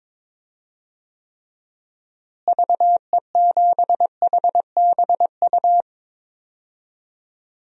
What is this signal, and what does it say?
Morse code, heard as a keyed tone, about 22 wpm, 700 Hz. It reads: VE7HBU
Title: DXCC-100 Call Signs - 6 Characters - Repeated 2x - 22wpm